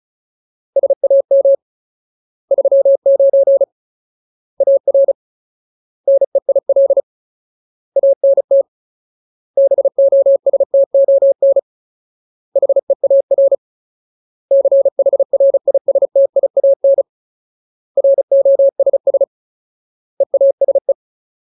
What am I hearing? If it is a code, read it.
Morse code, heard as a keyed tone, about 34 wpm, 550 Hz. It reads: SAM 39 AR NEIL ANT BOSTON HEAR CHRISTIAN ROSS EASE
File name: Ultimate Word List / 35WPM Ultimate